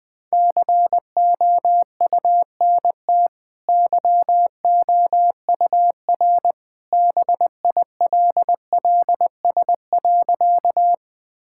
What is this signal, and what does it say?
Morse code, heard as a keyed tone, about 20 wpm, 700 Hz. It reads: COUNT YOUR BILLS.
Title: Easy Sentences with Hints - 20wpm